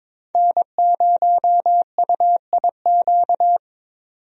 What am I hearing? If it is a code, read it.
Morse code, heard as a keyed tone, about 23 wpm, 700 Hz. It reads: N0UIQ
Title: USA Call Signs - 5 Characters - 22wpm